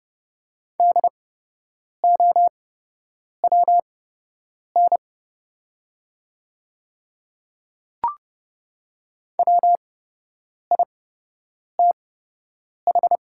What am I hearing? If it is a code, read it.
Morse code, heard as a keyed tone, about 30 wpm, spaced as for 8 wpm, 700 Hz. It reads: DOWN WITH
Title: 8x Character Spacing - Top 100 Words - 30wpm